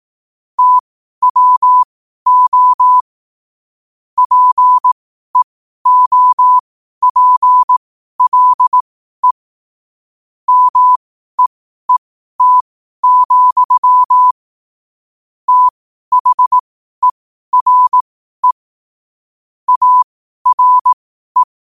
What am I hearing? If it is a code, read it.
Morse code, heard as a keyed tone, about 18 wpm, spaced as for 12 wpm, 1000 Hz. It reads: TWO PEOPLE MEET, THERE ARE
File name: qotdfiles / 12wpm